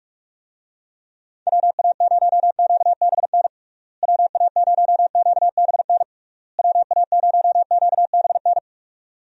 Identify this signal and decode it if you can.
Morse code, heard as a keyed tone, about 46 wpm, 700 Hz. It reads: WA0QBN WA0QBN WA0QBN
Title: USA Call Signs - 6 Characters - Repeated 3x - 45wpm